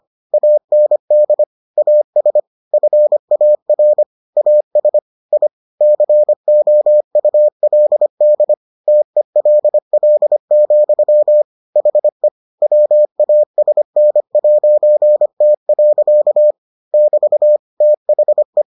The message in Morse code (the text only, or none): AND AS FAR AS I COULD TELL, HE WASN'T. <BT> THE